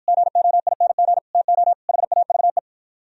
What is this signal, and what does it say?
Morse code, heard as a keyed tone, about 52 wpm, 700 Hz. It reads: GOING TO HAVE